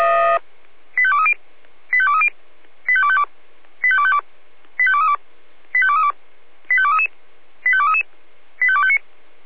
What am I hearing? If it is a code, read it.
Five-tone selective call (ZVEI), heard as quick runs of five tones, tones of about 70 ms. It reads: 85230 85230 85252 85252 85232 85232 85233 85233 85259